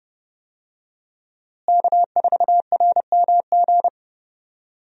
Morse code, heard as a keyed tone, about 30 wpm, 700 Hz. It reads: K4RMG
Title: USA Call Signs - 5 Characters - Repeated 3x - 30wpm